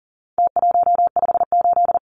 Morse code, heard as a keyed tone, about 40 wpm, 700 Hz. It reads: T158
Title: Numbers - Contest Style - 4-Digit 40wpm